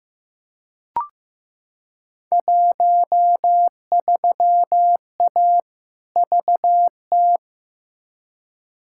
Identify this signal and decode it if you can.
Morse code, heard as a keyed tone, about 15 wpm, 700 Hz. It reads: 13A VT
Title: Field Day Exchanges Encore - 15wpm